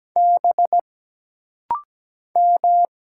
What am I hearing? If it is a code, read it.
Morse code, heard as a keyed tone, about 17 wpm, 700 Hz. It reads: B M
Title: Single Letters - Rapid-Fire 17wpm